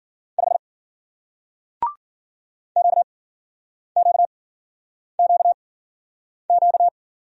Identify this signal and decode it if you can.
Morse code, heard as a keyed tone, about 53 wpm, spaced as for 9 wpm, 700 Hz. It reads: 5 QQQQ